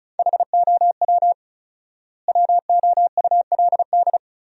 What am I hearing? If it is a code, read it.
Morse code, heard as a keyed tone, about 36 wpm, 700 Hz. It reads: HOW WOULD
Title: 4x Word-Spacing - Sentences from Top 300 Words - 35wpm